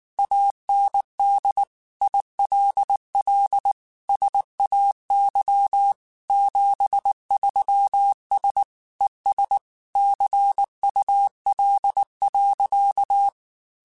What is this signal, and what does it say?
Morse code, heard as a keyed tone, about 19 wpm, 800 Hz. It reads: AND ILL SAY 73S ES CUL.
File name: QSOs / QSOs-mp3s / WA7QOX 2@18